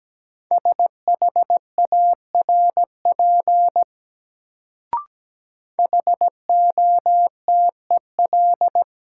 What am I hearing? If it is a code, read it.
Morse code, heard as a keyed tone, about 17 wpm, 700 Hz. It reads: SHARP HOTEL
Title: Sentences from Top 1600 Words - New Word Drill - 17wpm